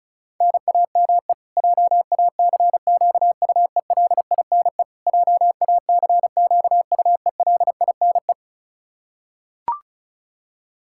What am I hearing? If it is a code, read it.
Morse code, heard as a keyed tone, about 34 wpm, 700 Hz. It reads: NAME JACQUELINE JACQUELINE
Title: Practice QSO - Exchange 1 - Beginning - 35wpm